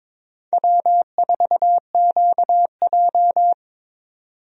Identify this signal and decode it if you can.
Morse code, heard as a keyed tone, about 22 wpm, 700 Hz. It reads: W4QJ
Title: USA Call Signs - 4 Characters - 22wpm